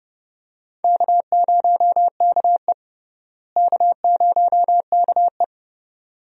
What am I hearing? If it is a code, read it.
Morse code, heard as a keyed tone, about 30 wpm, 700 Hz. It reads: K0KE K0KE
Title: USA Call Signs - 4 Characters - Repeated 2x - 30wpm